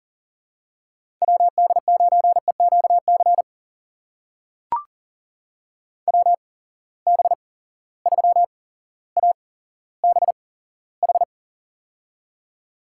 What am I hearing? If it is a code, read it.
Morse code, heard as a keyed tone, about 39 wpm, spaced as for 17 wpm, 700 Hz. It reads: WD9EQC WB3ABH
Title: USA Call Signs - 6 Characters - 8x Character Spacing - 40wpm